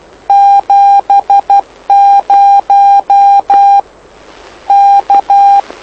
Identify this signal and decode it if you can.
Morse code, heard as a keyed tone, about 12 wpm, 800 Hz. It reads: 70 K
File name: lu4aao / lu7aa 09dic2011 exp planeador vuelo1